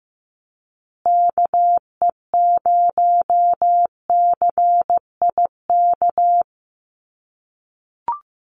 Morse code, heard as a keyed tone, about 15 wpm, 700 Hz. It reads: KE0CIK
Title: USA Call Signs Repeated 3x 15wpm